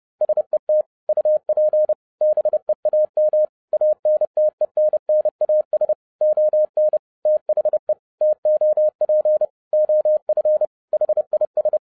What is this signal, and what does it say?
Morse code, heard as a keyed tone, about 30 wpm, 610 Hz. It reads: SET UP BEAM ANTENNAS ON THE TOP OF HIS